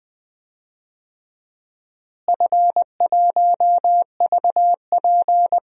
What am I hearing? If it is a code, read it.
Morse code, heard as a keyed tone, about 20 wpm, 700 Hz. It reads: F1VP
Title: DXCC-100 Call Signs - 4 Characters - Repeated 2x - 20wpm